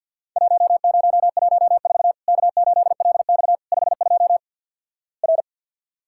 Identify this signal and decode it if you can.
Morse code, heard as a keyed tone, about 48 wpm, 700 Hz. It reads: 1014 K8LX 52 R
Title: Worked All Europe DX Contest - QTC Exchange - 50wpm